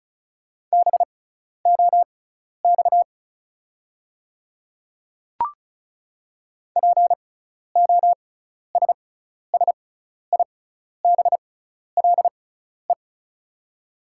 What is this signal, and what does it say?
Morse code, heard as a keyed tone, about 36 wpm, spaced as for 12 wpm, 690 Hz. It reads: BOX POSSIBLE
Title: 6x Character Spacing - Top 500 Words - 35wpm